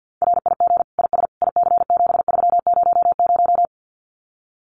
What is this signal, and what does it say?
Morse code, heard as a keyed tone, about 50 wpm, 700 Hz. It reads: RIG IS IC7300
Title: Common QSO Phrases Encore 50wpm